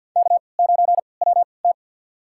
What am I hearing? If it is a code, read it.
Morse code, heard as a keyed tone, about 50 wpm, spaced as for 28 wpm, 700 Hz. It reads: K9WT
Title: USA Call Signs - 3x Character Spacing - 50wpm